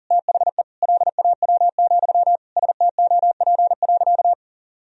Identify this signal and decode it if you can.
Morse code, heard as a keyed tone, about 40 wpm, 700 Hz. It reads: THE LAW, STOP.
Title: Sentences from Top 1000 Words - Review Part 3 - 40wpm